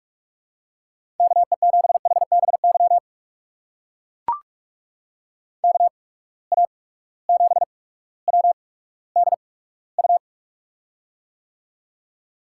Morse code, heard as a keyed tone, about 44 wpm, spaced as for 19 wpm, 700 Hz. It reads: KE7HBY KA7WDU
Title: USA Call Signs - 6 Characters - 8x Character Spacing - 45wpm